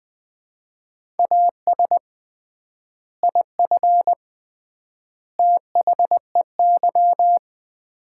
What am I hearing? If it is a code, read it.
Morse code, heard as a keyed tone, about 20 wpm, 700 Hz. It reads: AS IF THEY
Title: 3x Word-Spacing - Sets of 4 Words - 20wpm